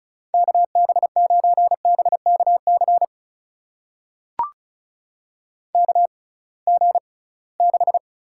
Morse code, heard as a keyed tone, about 36 wpm, spaced as for 23 wpm, 700 Hz. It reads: KB9BKC KG6
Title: USA Call Signs - 6 Characters - 6x Character Spacing - 35wpm